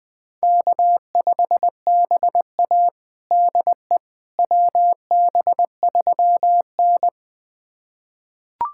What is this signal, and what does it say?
Morse code, heard as a keyed tone, about 20 wpm, 700 Hz. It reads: K5BA DE WB3N
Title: USA Callsign DE Callsign - 4 Characters - 20wpm